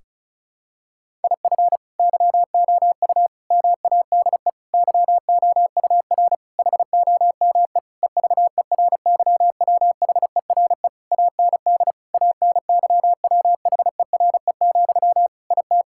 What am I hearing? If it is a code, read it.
Morse code, heard as a keyed tone, about 36 wpm, 700 Hz. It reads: IF YOU MADE YOUR HOME EVERYWHERE AND ANYWHERE, IT
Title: Sentences from Top 1600 Words 35wpm